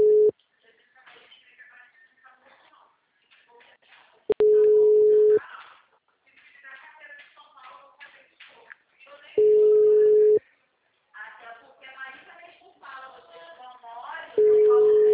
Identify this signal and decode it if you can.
Ringback tone (Europe, 425 Hz), with speech.
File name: out- -2053- - - .38